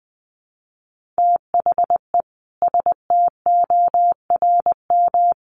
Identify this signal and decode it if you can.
Morse code, heard as a keyed tone, about 20 wpm, 700 Hz. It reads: THE STORM
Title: Sentences from Top 1500 Words 20wpm